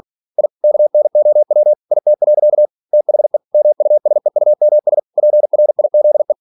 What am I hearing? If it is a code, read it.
Morse code, heard as a keyed tone, about 46 wpm, 600 Hz. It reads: I KNOW IT. THE MUSEUMS PRIZE
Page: Never 47wpm